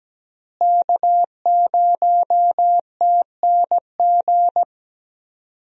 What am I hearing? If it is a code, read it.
Morse code, heard as a keyed tone, about 17 wpm, 700 Hz. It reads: K0TNG
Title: USA Call Signs - 5 Characters - Repeated 2x - 17wpm